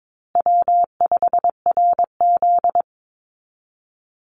Morse code, heard as a keyed tone, about 22 wpm, 700 Hz. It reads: W5RZ
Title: Easy Call Signs Repeated 2x - 22wpm